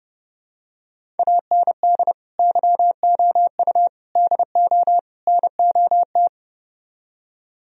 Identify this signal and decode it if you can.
Morse code, heard as a keyed tone, about 31 wpm, 700 Hz. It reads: AND YOU DO NOT